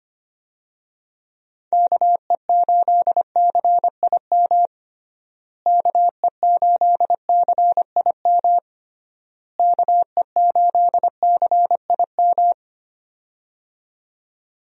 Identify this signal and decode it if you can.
Morse code, heard as a keyed tone, about 25 wpm, 700 Hz. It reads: KE8CIM KE8CIM KE8CIM
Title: USA Call Signs - 6 Characters - Repeated 3x - 25wpm